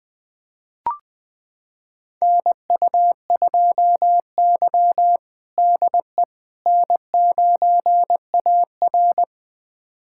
Morse code, heard as a keyed tone, about 20 wpm, 700 Hz. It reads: NU2Y DE N9AR
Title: USA Callsign DE Callsign - 4 Characters - 20wpm